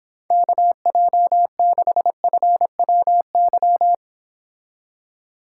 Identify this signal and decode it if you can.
Morse code, heard as a keyed tone, about 31 wpm, 700 Hz. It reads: KJ6FWY